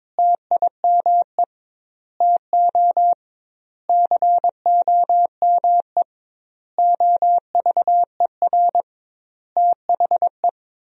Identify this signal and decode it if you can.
Morse code, heard as a keyed tone, about 22 wpm, 700 Hz. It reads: TIME TO COME OVER THE
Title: 2x Word-Spacing - Sentences from Top 300 Words - 22wpm